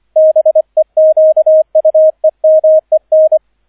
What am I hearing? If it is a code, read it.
Morse code, heard as a keyed tone, about 25 wpm, 630 Hz, with radio noise. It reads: BEQUEMEN